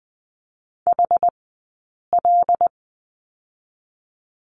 Morse code, heard as a keyed tone, about 20 wpm, 700 Hz. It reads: H L
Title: 086 Lesson 15d - Number 5 - Two Characters 20wpm